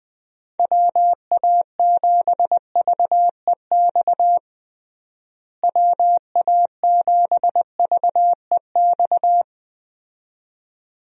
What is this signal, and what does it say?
Morse code, heard as a keyed tone, about 20 wpm, 700 Hz. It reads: WA7VEX WA7VEX
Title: USA Call Signs - 6 Characters - Repeated 2x - 20wpm